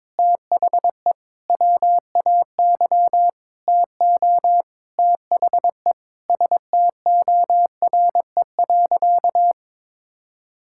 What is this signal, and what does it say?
Morse code, heard as a keyed tone, about 22 wpm, 700 Hz. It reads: THE WAY TO THE STORE.
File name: practice / 22wpm / Sentences with 8-Word Phrases - 22wpm